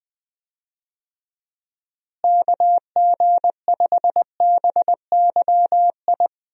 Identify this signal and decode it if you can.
Morse code, heard as a keyed tone, about 20 wpm, 700 Hz. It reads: KG5BYI
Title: USA Call Signs - 6 Characters - Repeated 3x - 20wpm